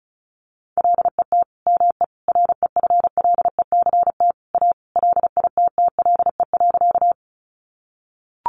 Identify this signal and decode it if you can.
Morse code, heard as a keyed tone, about 34 wpm, 700 Hz. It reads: LET ME REFLECT A LITTLE.
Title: Sentences from Top 1500 Words 35wpm